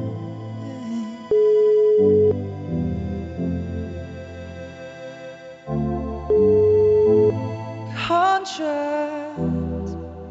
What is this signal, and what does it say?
Ringback tone (Europe, 425 Hz).